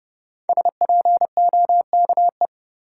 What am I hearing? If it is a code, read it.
Morse code, heard as a keyed tone, about 30 wpm, 700 Hz. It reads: SPOKE
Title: Sentences from Top 1200 Words - New Word Drill - 30wpm